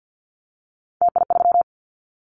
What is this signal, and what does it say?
Morse code, heard as a keyed tone, about 50 wpm, 700 Hz. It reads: TI3
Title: Callsign Prefixes for Top 100 DXCC - 50wpm